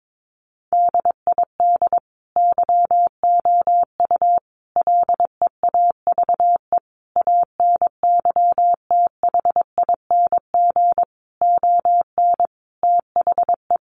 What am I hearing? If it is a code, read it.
Morse code, heard as a keyed tone, about 22 wpm, 700 Hz. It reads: DID YOU LEAVE ANYTHING ON THE
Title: Sentences from Top 1000 Words 22wpm